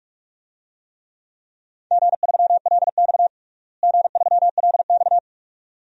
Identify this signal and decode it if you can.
Morse code, heard as a keyed tone, about 41 wpm, 700 Hz. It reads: G3LX G3LX
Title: DXCC-100 Call Signs - 4 Characters - Repeated 2x - 45wpm